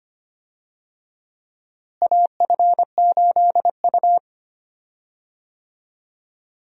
Morse code, heard as a keyed tone, about 25 wpm, 700 Hz. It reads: AF8U
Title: USA Call Signs - 4 Characters - Repeated 2x - 25wpm